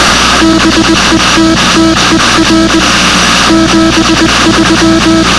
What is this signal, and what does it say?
Morse code, heard as a keyed tone, about 20 wpm, 320 Hz, with noise. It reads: BETTER 73